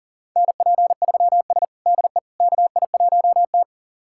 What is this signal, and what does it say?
Morse code, heard as a keyed tone, about 37 wpm, 700 Hz. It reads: NP3S DE KI1T
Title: USA Callsign DE Callsign - 4 Characters - 40wpm